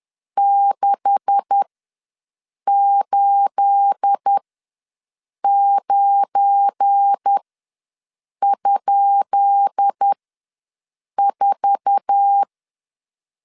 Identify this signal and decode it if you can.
Morse code, heard as a keyed tone, about 10 wpm, spaced as for 6 wpm, 800 Hz. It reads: 689?4
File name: LEZIONI / 10 SIGNS / 10B INT ws 10wpm 4134